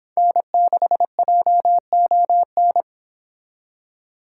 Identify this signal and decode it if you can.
Morse code, heard as a keyed tone, about 25 wpm, 700 Hz. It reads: N6JON